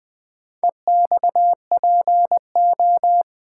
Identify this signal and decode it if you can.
Morse code, heard as a keyed tone, about 20 wpm, 700 Hz. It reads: EXPO
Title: Build A Word - 4 Letters At A Time - 8-Letter Words - 20wpm